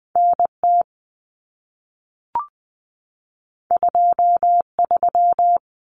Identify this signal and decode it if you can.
Morse code, heard as a keyed tone, about 20 wpm, 700 Hz. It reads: NT 23
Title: Common Cut Numbers - 2-Digit 20wpm